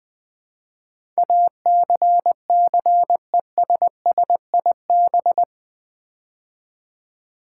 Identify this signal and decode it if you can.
Morse code, heard as a keyed tone, about 20 wpm, 700 Hz. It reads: ACCESSIB
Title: Build A Word - 4 Letters At A Time - 10-Letter Words - 20wpm